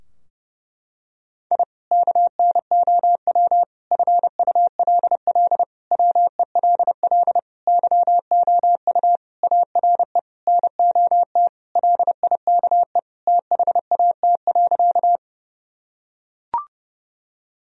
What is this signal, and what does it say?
Morse code, heard as a keyed tone, about 30 wpm, 700 Hz. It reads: I KNOW FULL WELL YOU ARE NOT LIKE THAT.
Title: Sentences from Top 1000 Words - Review Part 3 - 30wpm